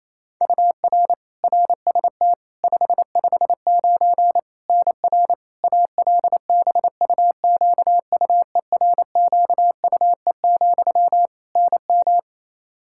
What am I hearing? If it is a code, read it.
Morse code, heard as a keyed tone, about 27 wpm, 700 Hz. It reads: UR RST 559 NR ALBUQUERQUE, NM
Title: Practice QSO - Exchange 1 - Beginning - 28wpm